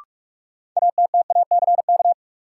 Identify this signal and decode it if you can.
Morse code, heard as a keyed tone, about 43 wpm, 700 Hz. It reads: ATTACK
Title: Sentences from Top 1500 Words - New Word Drill Alt - 45wpm